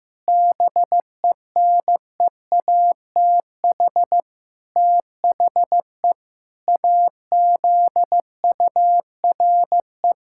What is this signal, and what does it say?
Morse code, heard as a keyed tone, about 15 wpm, 700 Hz. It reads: BENEATH THE AZURE
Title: Pangram Challenge v2 - 15wpm